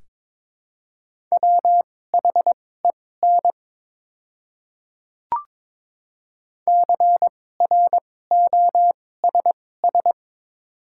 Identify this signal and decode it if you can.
Morse code, heard as a keyed tone, about 21 wpm, spaced as for 16 wpm, 700 Hz. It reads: WHEN CROSS
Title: 2x Character Spacing - Top 500 Words - 22wpm